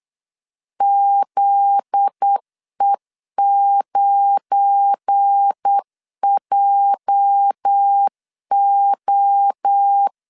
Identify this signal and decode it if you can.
Morse code, heard as a keyed tone, about 8 wpm, 800 Hz. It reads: ZE9JO
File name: LEZIONI / 8 YJ45 / 8 YJ 8wpm 3313